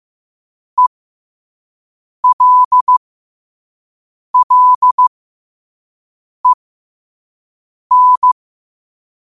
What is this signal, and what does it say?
Morse code, heard as a keyed tone, about 15 wpm, spaced as for 5 wpm, 1000 Hz. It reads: ELLEN
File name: qotdfiles / 5wpm